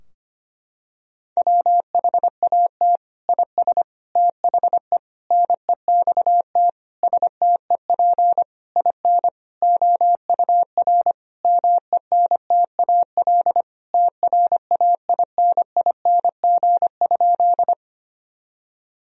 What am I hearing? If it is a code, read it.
Morse code, heard as a keyed tone, about 25 wpm, 700 Hz. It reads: WHAT IS THE NEXT STEP IN OUR MENTAL TRAINING?
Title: Sentences from Top 1000 Words - Review Part 4 - 25wpm